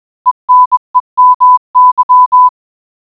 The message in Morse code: ENEMY